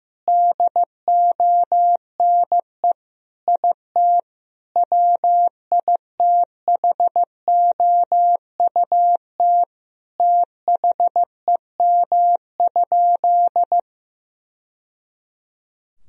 Morse code, heard as a keyed tone, about 15 wpm, 700 Hz. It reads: DONE IT WITHOUT THEM?
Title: Sentences from Top 1000 Words - Review Part 3 - 15wpm